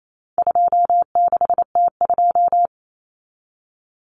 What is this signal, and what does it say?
Morse code, heard as a keyed tone, about 28 wpm, 700 Hz. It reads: 26T2
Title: Common Cut Numbers - 4-Digit 28wpm